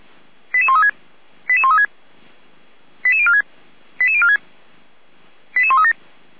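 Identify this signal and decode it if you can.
Five-tone selective call (ZVEI), heard as quick runs of five tones, tones of about 70 ms. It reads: 88136 88136 88046 88046 88137
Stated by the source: ZVEI five-tone selective call 88407, received by SDR.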